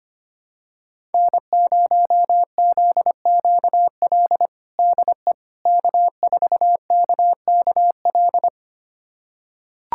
Morse code, heard as a keyed tone, about 25 wpm, 700 Hz. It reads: N0ZQL DE K4KKL
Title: USA Callsign DE Callsign - 5 Characters - 25wpm